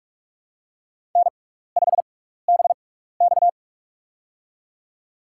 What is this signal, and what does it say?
Morse code, heard as a keyed tone, about 47 wpm, spaced as for 16 wpm, 700 Hz. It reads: N5BX
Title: USA Call Signs - 4 Characters - 6x Character Spacing - 45wpm